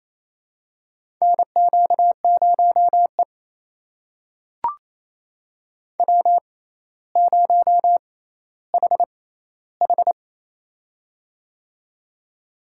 Morse code, heard as a keyed tone, about 27 wpm, spaced as for 14 wpm, 700 Hz. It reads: NQ0E W0HH
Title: USA Call Signs - 4 Characters - 6x Character Spacing - 28wpm